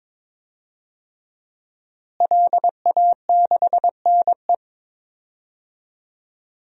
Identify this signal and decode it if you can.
Morse code, heard as a keyed tone, about 23 wpm, 700 Hz. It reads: LA6NE
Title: DXCC-100 Call Signs - 5 Characters - Repeated 3x - 22wpm